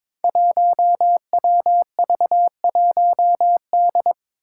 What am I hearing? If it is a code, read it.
Morse code, heard as a keyed tone, about 22 wpm, 700 Hz. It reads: 1WV1D
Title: Five Challenging Letter-Number Combinations 22wpm